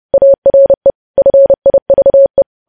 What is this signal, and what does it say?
Morse code, heard as a keyed tone, about 30 wpm, 550 Hz. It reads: ARE FIVE
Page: news headlines in morse code at 30 wpm